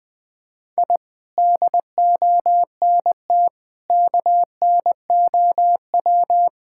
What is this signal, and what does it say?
Morse code, heard as a keyed tone, about 20 wpm, 700 Hz. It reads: I DONT KNOW
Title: Sentences with 6-Word Phrases - 20wpm